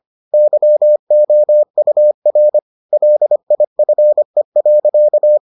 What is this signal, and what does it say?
Morse code, heard as a keyed tone, about 25 wpm, 600 Hz. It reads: YOUR LIFE.